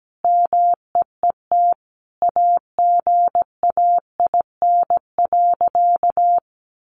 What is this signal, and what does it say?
Morse code, heard as a keyed tone, about 17 wpm, 700 Hz. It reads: MEET AGAIN.
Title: Sentences from Top 1100 Words 17wpm